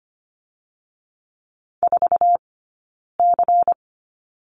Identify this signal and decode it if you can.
Morse code, heard as a keyed tone, about 25 wpm, 700 Hz. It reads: 4 C
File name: course / 25:10wpm / 092 Lesson 16d - Letter c - Two Characters 25wpm